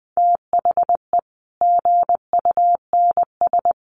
Morse code, heard as a keyed tone, about 20 wpm, 700 Hz. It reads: THE GUNS
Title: A Sound Of Thunder 20wpm